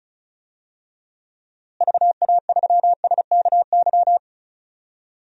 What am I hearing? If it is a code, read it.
Morse code, heard as a keyed tone, about 36 wpm, 700 Hz. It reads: VA3SKY
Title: DXCC-100 Call Signs - 6 Characters - Repeated 3x - 35wpm